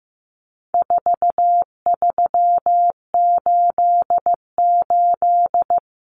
Morse code, heard as a keyed tone, about 15 wpm, 700 Hz. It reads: 4388
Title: Common Cut Numbers - 4-Digit 15wpm